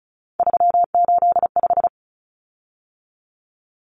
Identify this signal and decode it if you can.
Morse code, heard as a keyed tone, about 34 wpm, 700 Hz. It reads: 385
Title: Common Cut Numbers - 3-Digit 35wpm